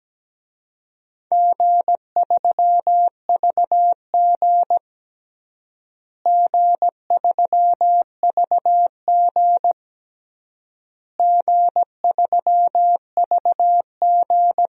Morse code, heard as a keyed tone, about 17 wpm, 700 Hz. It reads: G3VG G3VG G3VG
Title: DXCC-100 Call Signs - 4 Characters - Repeated 3x - 17wpm